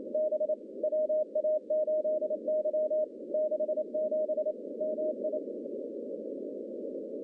Morse code, heard as a keyed tone, about 28 wpm, 620 Hz, with noise. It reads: B WA8Y 67 MI
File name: contests / CW-SS-k5tr- -1200